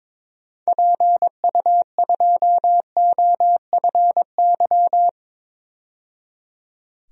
Morse code, heard as a keyed tone, about 21 wpm, 700 Hz. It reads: PU2OFY
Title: DXCC-100 Call Signs - 6 Characters - 22wpm